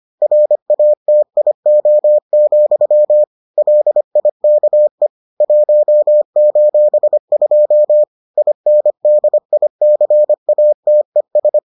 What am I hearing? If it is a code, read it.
Morse code, heard as a keyed tone, about 25 wpm, 600 Hz. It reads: RATIO, LIKE 1:2 INDICATES